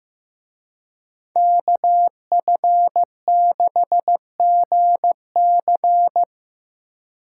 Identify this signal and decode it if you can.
Morse code, heard as a keyed tone, about 17 wpm, 700 Hz. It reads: KF6GC
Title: Speed Racing - USA 2x2 Call Signs 15wpm